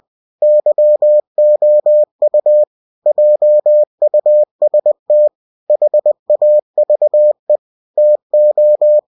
Morse code, heard as a keyed tone, about 20 wpm, 600 Hz. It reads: YOU JUST HAVE TO
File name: cw-books / winnie-the-pooh / winnie-20wpm-page-0065